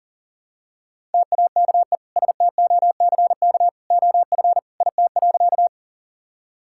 Morse code, heard as a keyed tone, about 40 wpm, 700 Hz. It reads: TAKE STOCK OF IT.
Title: Sentences from Top 1000 Words - Review Part 5 - 40wpm